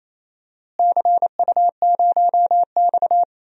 Morse code, heard as a keyed tone, about 28 wpm, 700 Hz. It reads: CU0X
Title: DXCC-100 Call Signs - 4 Characters - Repeated 2x - 28wpm